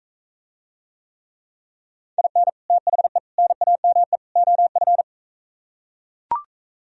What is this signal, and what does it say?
Morse code, heard as a keyed tone, about 45 wpm, 700 Hz. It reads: IN THE NAME OF